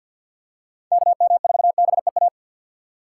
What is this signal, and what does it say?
Morse code, heard as a keyed tone, about 49 wpm, 700 Hz. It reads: KM4BEA
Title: USA Call Signs - 3x Character Spacing - 50wpm